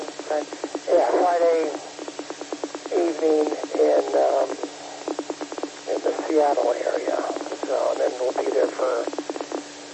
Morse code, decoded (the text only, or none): none